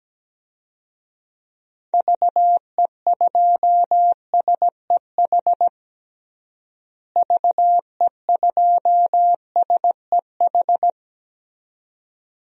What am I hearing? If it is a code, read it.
Morse code, heard as a keyed tone, about 17 wpm, 700 Hz. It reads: VE2SEH VE2SEH
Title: DXCC-100 Call Signs - 6 Characters - Repeated 2x - 17wpm